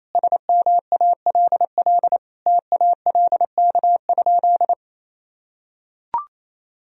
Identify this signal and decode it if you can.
Morse code, heard as a keyed tone, about 28 wpm, 700 Hz. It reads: SMALL TALK?
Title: Sentences from Top 1000 Words - Review Part 1 - 28wpm